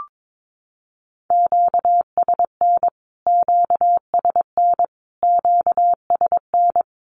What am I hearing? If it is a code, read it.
Morse code, heard as a keyed tone, about 22 wpm, 700 Hz. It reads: QSN QSN QSN
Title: Q-Codes Repeated 3x 22wpm